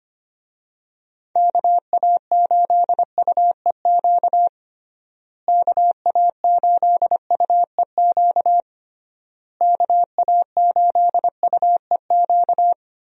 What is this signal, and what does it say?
Morse code, heard as a keyed tone, about 25 wpm, 700 Hz. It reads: KA8UEQ KA8UEQ KA8UEQ
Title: USA Call Signs - 6 Characters - Repeated 3x - 25wpm